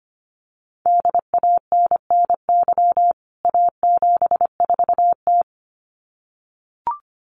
Morse code, heard as a keyed tone, about 25 wpm, 700 Hz. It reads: DANNY A74T
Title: CWT Exchange 25wpm